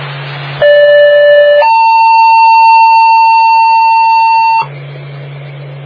Two-tone page, heard as a one-second tone, then a three-second tone, about 580 then 900 Hz. It is heard from 0.6 to 4.6 s.